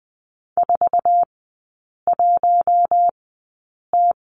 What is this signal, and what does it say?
Morse code, heard as a keyed tone, about 20 wpm, spaced as for 8 wpm, 700 Hz. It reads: 41T
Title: 081 Lesson 14e - Number 2 - Three Characters 20wpm